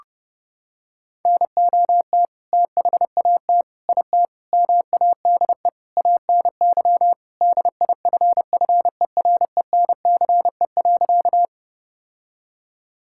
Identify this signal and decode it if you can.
Morse code, heard as a keyed tone, about 30 wpm, 700 Hz. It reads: NOT THAT IT MADE ANY DIFFERENCE.
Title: Sentences from Top 1000 Words - Review Part 5 - 30wpm